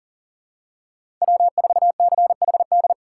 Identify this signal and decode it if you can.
Morse code, heard as a keyed tone, about 40 wpm, 700 Hz. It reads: W4CHD
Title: USA Call Signs - 5 Characters - 6x Character Spacing - 40wpm